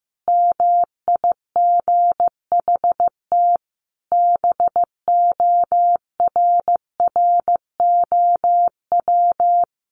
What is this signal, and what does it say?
Morse code, heard as a keyed tone, about 15 wpm, 700 Hz. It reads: MIGHT BORROW
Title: Sentences from Top 1500 Words 15wpm